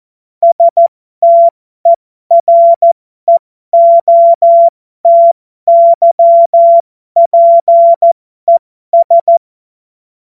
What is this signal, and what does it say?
Morse code, heard as a keyed tone, about 14 wpm, spaced as for 12 wpm, 680 Hz. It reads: STEREOTYPES
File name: Morsekurs / Witze